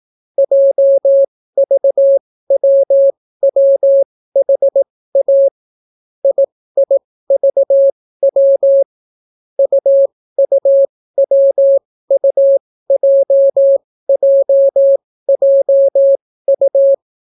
Morse code, heard as a keyed tone, about 18 wpm, spaced as for 14 wpm, 550 Hz. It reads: JVWWHA IIVW UUWUJJJU